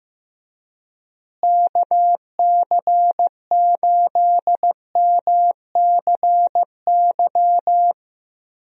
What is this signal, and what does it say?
Morse code, heard as a keyed tone, about 15 wpm, 700 Hz. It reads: KC8MCY
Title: USA Call Signs - 6 Characters - Repeated 3x - 15wpm